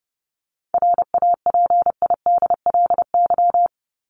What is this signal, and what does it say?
Morse code, heard as a keyed tone, about 30 wpm, 700 Hz. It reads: RAPIDLY
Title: LY Suffix - Top 100 Words with Base Words 30wpm